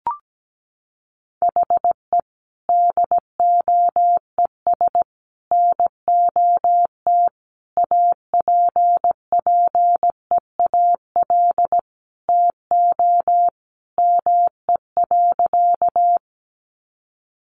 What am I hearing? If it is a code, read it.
Morse code, heard as a keyed tone, about 17 wpm, 700 Hz. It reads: HE DOES NOT APPEAL TO ME.
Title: Sentences from Top 1300 Words 17wpm